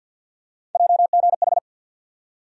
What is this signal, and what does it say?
Morse code, heard as a keyed tone, about 52 wpm, 700 Hz. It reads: JGH
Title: USA Call Sign Postfixes - 3 Letters 50wpm